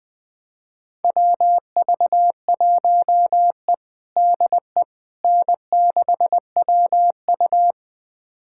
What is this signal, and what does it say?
Morse code, heard as a keyed tone, about 20 wpm, 700 Hz. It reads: WV1E DE N6WU
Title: USA Callsign DE Callsign - 4 Characters - 20wpm